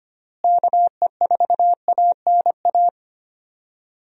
Morse code, heard as a keyed tone, about 25 wpm, 700 Hz. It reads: KE4ANA
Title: USA Call Signs - 6 Characters - Repeated 3x - 25wpm